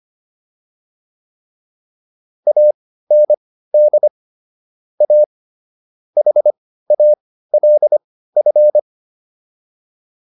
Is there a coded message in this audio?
Yes: Morse code from 2.5 to 8.8 s, heard as a keyed tone, about 25 wpm, spaced as for 15 wpm, 600 Hz. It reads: AND A HALF